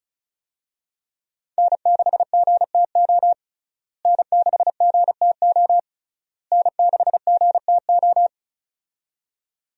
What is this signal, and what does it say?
Morse code, heard as a keyed tone, about 36 wpm, 700 Hz. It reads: N6GTO N6GTO N6GTO
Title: USA Call Signs - 5 Characters - Repeated 3x - 35wpm